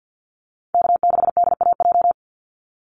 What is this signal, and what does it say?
Morse code, heard as a keyed tone, about 50 wpm, 700 Hz. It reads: K6DAJ
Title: Call Signs Repeated 2x - 50wpm